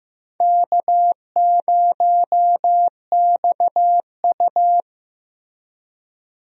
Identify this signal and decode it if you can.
Morse code, heard as a keyed tone, about 15 wpm, 700 Hz. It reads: K0XU
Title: USA Call Signs - 4 Characters - Repeated 2x - 15wpm